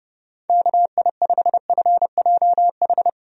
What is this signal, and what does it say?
Morse code, heard as a keyed tone, about 32 wpm, 700 Hz. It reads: KI5FJH
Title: USA Call Signs - 6 Characters - 30wpm